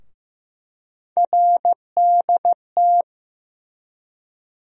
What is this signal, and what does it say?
Morse code, heard as a keyed tone, about 15 wpm, 700 Hz. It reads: RDT
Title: USA Call Sign Postfixes - 3 Letters 15wpm